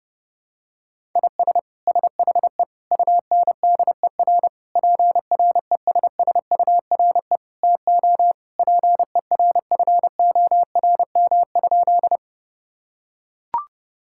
Morse code, heard as a keyed tone, about 30 wpm, 700 Hz. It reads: IS SHE UNDER PRESSURE TO PERFORM?
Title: Sentences from Top 1000 Words - Review Part 6 - 30wpm